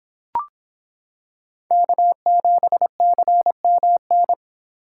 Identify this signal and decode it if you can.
Morse code, heard as a keyed tone, about 25 wpm, 700 Hz. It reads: K7CMN